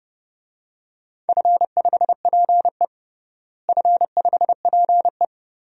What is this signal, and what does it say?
Morse code, heard as a keyed tone, about 32 wpm, 700 Hz. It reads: F5PE F5PE
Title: DXCC-100 Call Signs - 4 Characters - Repeated 2x - 30wpm